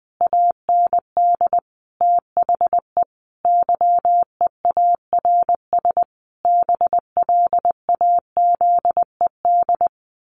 Morse code, heard as a keyed tone, about 20 wpm, 700 Hz. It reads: AND THE YEARS BLAZED
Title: A Sound Of Thunder 20wpm